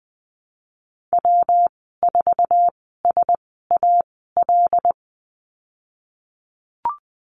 Morse code, heard as a keyed tone, about 20 wpm, spaced as for 15 wpm, 700 Hz. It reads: W4SAL
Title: 161 Lesson 26g - Letter g - Call Signs 20wpm